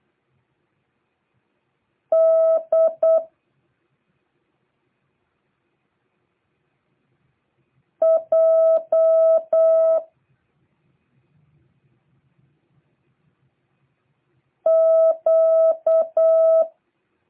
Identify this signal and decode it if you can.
Morse code, heard as a keyed tone, about 8 wpm, 650 Hz. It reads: D J Q